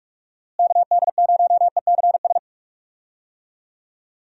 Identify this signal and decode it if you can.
Morse code, heard as a keyed tone, about 46 wpm, 700 Hz. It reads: KD0ECS
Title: USA Call Signs - 6 Characters - Repeated 3x - 45wpm